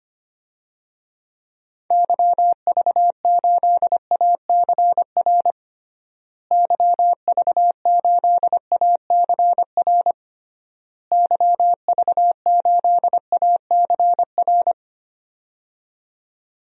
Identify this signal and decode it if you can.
Morse code, heard as a keyed tone, about 25 wpm, 700 Hz. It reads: YV8ACR YV8ACR YV8ACR
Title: DXCC-100 Call Signs - 6 Characters - Repeated 3x - 25wpm